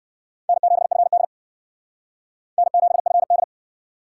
Morse code, heard as a keyed tone, about 56 wpm, 700 Hz. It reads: N7FD N7FD
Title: Speed Racing - USA 1x2 Call Signs 45wpm